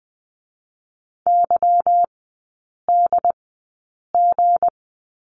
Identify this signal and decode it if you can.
Morse code, heard as a keyed tone, about 20 wpm, spaced as for 8 wpm, 700 Hz. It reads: YDG
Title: 159 Lesson 26e - Letter g - Three Characters 20wpm